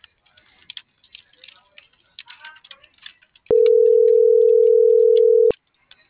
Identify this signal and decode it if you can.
Ringback tone (North America), 440 + 480 Hz.